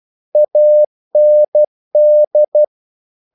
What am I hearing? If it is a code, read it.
Morse code, heard as a keyed tone, about 12 wpm, 600 Hz. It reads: AND
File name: CW / Practice / If12